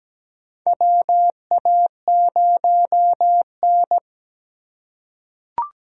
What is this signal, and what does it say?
Morse code, heard as a keyed tone, about 17 wpm, 700 Hz. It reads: WA0N
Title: USA Call Signs - 4 Characters - 4x Character Spacing - 17wpm